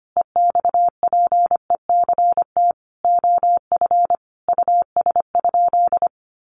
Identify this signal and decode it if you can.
Morse code, heard as a keyed tone, about 25 wpm, 700 Hz. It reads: EXPECT OF US?
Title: Sentences from Top 1400 Words 25wpm